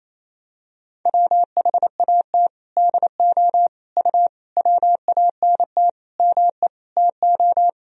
Morse code, heard as a keyed tone, about 28 wpm, 700 Hz. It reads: WHAT DO U WANT ME TO
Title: Sentences from Top 200 Words and Abbreviations 28wpm